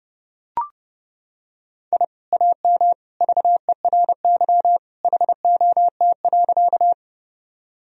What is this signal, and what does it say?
Morse code, heard as a keyed tone, about 31 wpm, 700 Hz. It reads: I AM VERY HOT.
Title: Sentences from Top 1000 Words - Review Part 3 - 30wpm